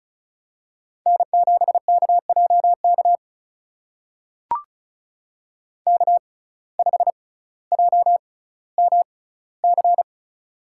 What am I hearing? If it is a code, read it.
Morse code, heard as a keyed tone, about 34 wpm, spaced as for 18 wpm, 700 Hz. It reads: N7KJK K5JMC
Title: USA Call Signs - 5 Characters - 6x Character Spacing - 35wpm